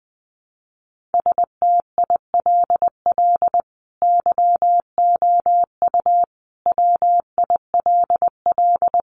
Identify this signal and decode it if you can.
Morse code, heard as a keyed tone, about 20 wpm, 700 Hz. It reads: STILL YOU WILL
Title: Sentences from Top 600 Words 20wpm